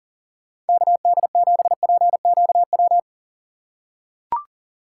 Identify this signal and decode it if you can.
Morse code, heard as a keyed tone, about 40 wpm, 700 Hz. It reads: KD7PQW
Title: USA Call Signs - 6 Characters - 4x Character Spacing - 40wpm